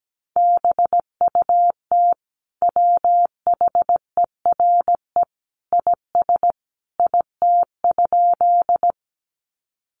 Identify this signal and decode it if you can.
Morse code, heard as a keyed tone, about 17 wpm, 700 Hz. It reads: BUT WHERE IS IT?
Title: Sentences from Top 200 Words 17wpm